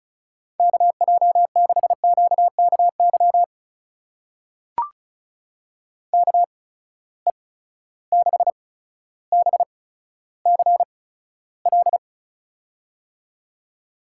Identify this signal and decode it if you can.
Morse code, heard as a keyed tone, about 36 wpm, spaced as for 15 wpm, 700 Hz. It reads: KJ6QKY KE6BCL
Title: USA Call Signs - 6 Characters - 8x Character Spacing - 35wpm